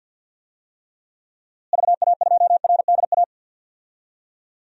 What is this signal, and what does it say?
Morse code, heard as a keyed tone, about 51 wpm, 700 Hz. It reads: VA2RDA
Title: DXCC-100 Call Signs - 6 Characters - Repeated 3x - 50wpm